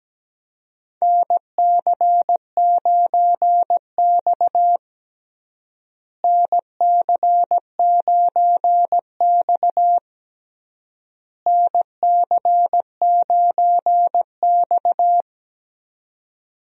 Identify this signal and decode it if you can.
Morse code, heard as a keyed tone, about 17 wpm, 700 Hz. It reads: NC9X NC9X NC9X
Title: USA Call Signs - 4 Characters - Repeated 3x - 17wpm